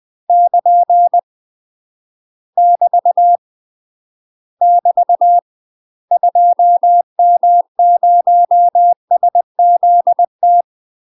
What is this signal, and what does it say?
Morse code, heard as a keyed tone, about 20 wpm, 700 Hz. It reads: <KN> <BT> <BT> 2M0SZT
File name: UK MASTER QSOs TRAINING / 20WPM / UK MASTER QSOs 0042